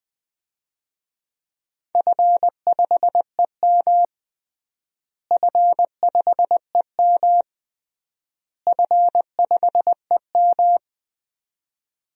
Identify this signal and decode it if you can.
Morse code, heard as a keyed tone, about 20 wpm, 700 Hz. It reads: F5EM F5EM F5EM
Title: DXCC-100 Call Signs - 4 Characters - Repeated 3x - 20wpm